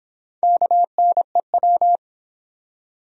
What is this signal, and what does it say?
Morse code, heard as a keyed tone, about 24 wpm, 700 Hz. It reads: KNEW